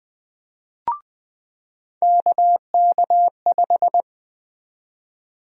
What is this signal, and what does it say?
Morse code, heard as a keyed tone, about 20 wpm, 700 Hz. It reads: KK5
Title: USA Call Sign Prefixes - 20wpm